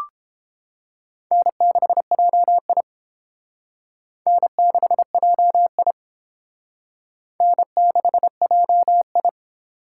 Morse code, heard as a keyed tone, about 29 wpm, 700 Hz. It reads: N6JI N6JI N6JI